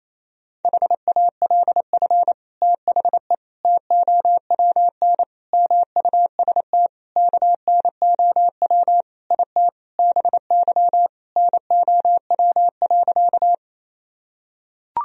Morse code, heard as a keyed tone, about 28 wpm, 700 Hz. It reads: HALF THE TOWN MUST KNOW IT BY NOW.
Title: Sentences from Top 1000 Words - Review Part 2 - 28wpm